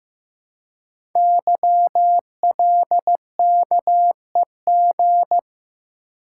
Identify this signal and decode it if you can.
Morse code, heard as a keyed tone, about 15 wpm, 700 Hz. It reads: YLKEG
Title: Code Groups - 5 Letters - 15wpm